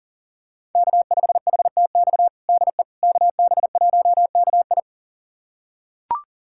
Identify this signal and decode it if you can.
Morse code, heard as a keyed tone, about 40 wpm, 700 Hz. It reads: K5HTX DE KB1KI
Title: USA Callsign DE Callsign - 5 Characters - 40wpm